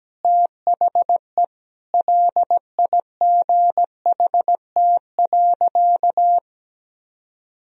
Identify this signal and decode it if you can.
Morse code, heard as a keyed tone, about 17 wpm, 700 Hz. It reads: THE LIGHT.
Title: Sentences from Top 1000 Words - Review Part 1 - 17wpm